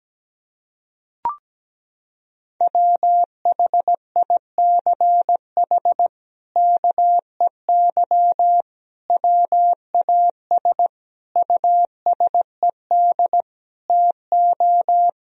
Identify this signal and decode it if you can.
Morse code, heard as a keyed tone, about 17 wpm, 700 Hz. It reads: WHICH KEY WAS USED TO
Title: Sentences from Top 1000 Words - Review Part 4 - 17wpm